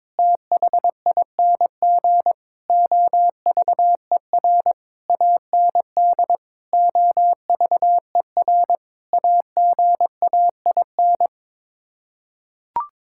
Morse code, heard as a keyed tone, about 21 wpm, 700 Hz. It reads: THING OVER AND OVER AGAIN
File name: practice / 22wpm / Sets of 8 Words - 22wpm